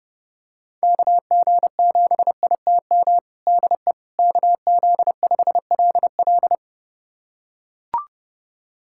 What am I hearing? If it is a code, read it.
Morse code, heard as a keyed tone, about 32 wpm, 700 Hz. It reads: KG7ITM DE KZ5LL
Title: USA Callsign DE Callsign 30wpm